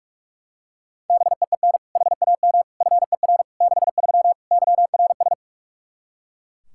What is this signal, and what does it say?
Morse code, heard as a keyed tone, about 42 wpm, 700 Hz. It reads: BEEN HAM FER 63 YRS
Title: Practice QSO - Exchange 3 - More Info - 45wpm